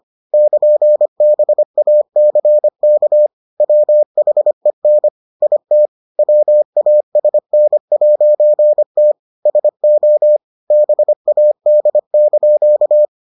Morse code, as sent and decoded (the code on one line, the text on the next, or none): -.--. -... .- -.-. -.- / .-- .... -. / .. - / .-- .- ... -. .----. - / ... --- / -... .- -.. -.--.-
<KN>BACK WHEN IT WASN'T SO BAD)